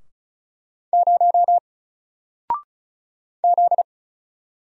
Morse code, heard as a keyed tone, about 33 wpm, 700 Hz. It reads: 0 Z
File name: Single-Letter-Number-35wpm